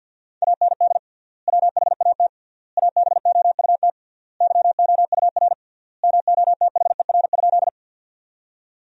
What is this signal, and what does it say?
Morse code, heard as a keyed tone, about 50 wpm, 700 Hz. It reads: AND WHAT ABOUT YOUR MOTHER?
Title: 3x Word-Spacing - Sentences from Top 200 Words - 50wpm